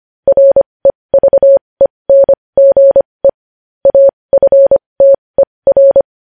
Morse code, heard as a keyed tone, about 25 wpm, spaced as for 20 wpm, 550 Hz. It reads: REVENGE AFTER
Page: news-headlines-in-morse-code-at-20-wpm